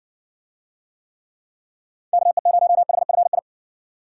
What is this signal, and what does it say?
Morse code, heard as a keyed tone, about 60 wpm, 700 Hz. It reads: KE0HFI